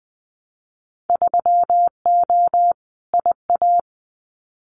Morse code, heard as a keyed tone, about 20 wpm, 700 Hz. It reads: 3O IA
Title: Winter Field Day Exchanges 20wpm